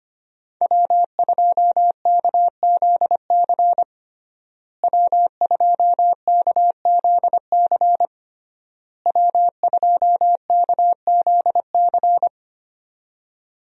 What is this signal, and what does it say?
Morse code, heard as a keyed tone, about 25 wpm, 700 Hz. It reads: W2KZC W2KZC W2KZC
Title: USA Call Signs - 5 Characters - Repeated 3x - 25wpm